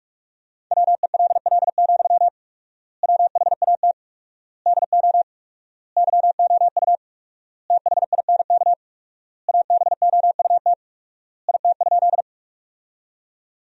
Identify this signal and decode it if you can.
Morse code, heard as a keyed tone, about 41 wpm, 700 Hz. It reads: WELL, WHAT DO YOU THINK ABOUT IT?
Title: 4x Word-Spacing - Sentences from Top 200 Words - 45wpm